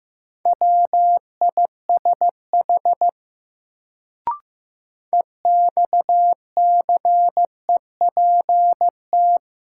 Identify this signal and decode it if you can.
Morse code, heard as a keyed tone, about 15 wpm, 700 Hz. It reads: WISH EXCEPT
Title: Sentences from Top 1600 Words - New Word Drill - 15wpm